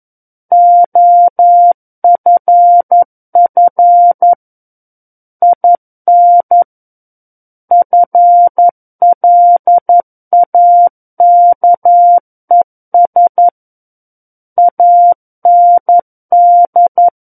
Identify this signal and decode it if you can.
Morse code, heard as a keyed tone, about 11 wpm, 700 Hz. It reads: OFF IN FLAKES AND
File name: 11 WPM